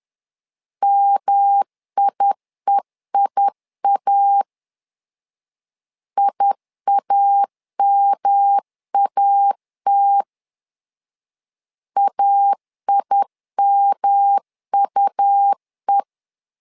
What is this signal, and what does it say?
Morse code, heard as a keyed tone, about 11 wpm, 800 Hz. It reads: MIEIA IAMAT AIMUE
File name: LEZIONI / 3 MWRO / 3 M 10wpm 1828